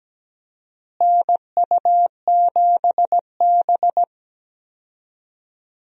Morse code, heard as a keyed tone, about 17 wpm, 700 Hz. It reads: NU7B
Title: USA Call Signs - 4 Characters - Repeated 3x - 17wpm